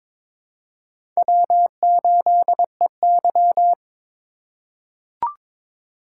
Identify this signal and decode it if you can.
Morse code, heard as a keyed tone, about 22 wpm, 700 Hz. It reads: W8EY
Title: USA Call Signs - 4 Characters - 2x Character Spacing - 22wpm